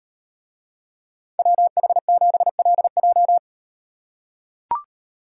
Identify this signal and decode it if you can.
Morse code, heard as a keyed tone, about 37 wpm, 700 Hz. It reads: WH7LJ